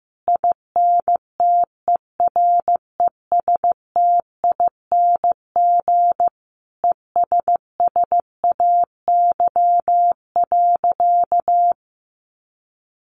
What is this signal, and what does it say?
Morse code, heard as a keyed tone, about 15 wpm, 700 Hz. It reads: INTERESTING ESSAY.
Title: Sentences from Top 1500 Words 15wpm